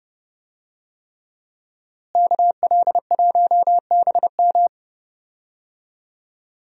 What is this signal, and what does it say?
Morse code, heard as a keyed tone, about 30 wpm, 700 Hz. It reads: KL1BM
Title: USA Call Signs - 5 Characters - Repeated 3x - 30wpm